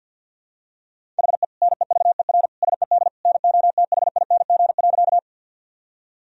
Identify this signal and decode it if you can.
Morse code, heard as a keyed tone, about 45 wpm, 700 Hz. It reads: HE NEVER SED NOTHING.
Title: Sentences from Top 500 Words and Abbreviations 50wpm